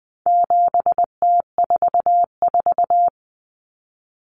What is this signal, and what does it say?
Morse code, heard as a keyed tone, about 20 wpm, 700 Hz. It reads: 7T44
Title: Common Cut Numbers - 4-Digit 20wpm